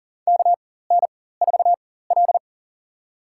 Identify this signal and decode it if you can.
Morse code, heard as a keyed tone, about 40 wpm, spaced as for 19 wpm, 700 Hz. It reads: KN4L
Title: USA Call Signs - 4 Characters - 4x Character Spacing - 40wpm